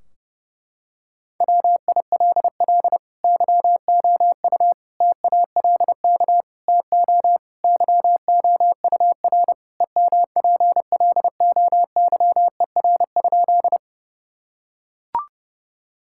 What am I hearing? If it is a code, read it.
Morse code, heard as a keyed tone, about 30 wpm, 700 Hz. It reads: WILL YOU TALK TO YOUR EMPLOYER?
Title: Sentences from Top 1600 Words 30wpm